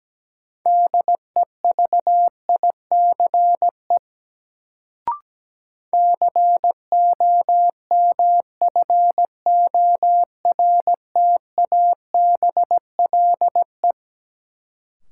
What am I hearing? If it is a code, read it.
Morse code, heard as a keyed tone, about 17 wpm, 700 Hz. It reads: DEVICE COMFORTABLE
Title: Sentences from Top 1500 Words - New Word Drill Alt - 17wpm